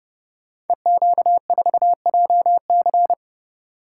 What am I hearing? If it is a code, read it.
Morse code, heard as a keyed tone, about 30 wpm, 700 Hz. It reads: EQ4JC
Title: Code Groups - 5 Letter:Numbers - 30wpm